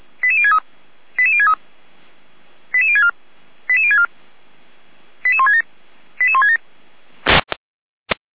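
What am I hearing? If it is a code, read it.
Five-tone selective call (ZVEI), heard as quick runs of five tones, tones of about 70 ms. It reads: 88063 88063 88064 88064 88167 88167
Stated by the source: ZVEI five-tone call 88365 received by SDR.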